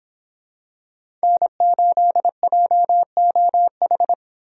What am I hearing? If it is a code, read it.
Morse code, heard as a keyed tone, about 26 wpm, 700 Hz. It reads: N8JOH
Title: Speed Racing - USA 1x3 Call Signs 22wpm